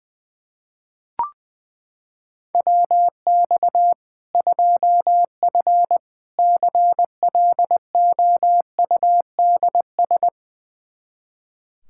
Morse code, heard as a keyed tone, about 20 wpm, 700 Hz. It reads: WX 2F CLOUDS
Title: Practice QSO - Exchange 2 - Info - 20wpm